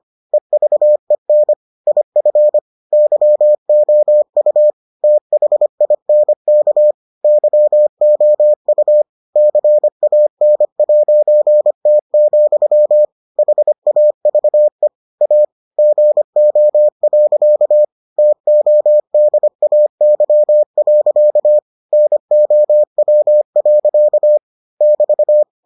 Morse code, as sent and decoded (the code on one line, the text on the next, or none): . ...- . -. / .. ..-. / -.-- --- ..- / - .... .. -. -.- / -.-- --- ..- / -.-. .- -. .----. - --..-- / .... .- ...- . / .- / --. --- .-.-.- / - --- -.. .- -.-- .-.-.- / -. --- .-- .-.-.- / -...-
EVEN IF YOU THINK YOU CAN'T, HAVE A GO. TODAY. NOW. <BT>